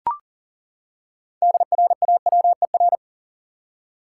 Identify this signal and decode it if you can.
Morse code, heard as a keyed tone, about 40 wpm, 700 Hz. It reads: DRAWER
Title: Six Letter Words - 40wpm